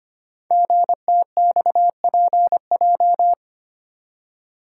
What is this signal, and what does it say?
Morse code, heard as a keyed tone, about 25 wpm, 700 Hz. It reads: GTXPJ
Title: Code Groups - 5 Letters - 25wpm